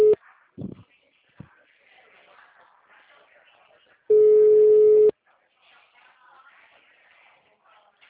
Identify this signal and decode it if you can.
Ringback tone (Europe, 425 Hz).
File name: out- -2009- - - .52163